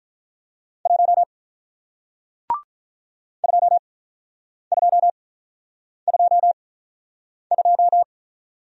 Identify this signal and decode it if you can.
Morse code, heard as a keyed tone, about 46 wpm, spaced as for 9 wpm, 700 Hz. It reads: 1 2222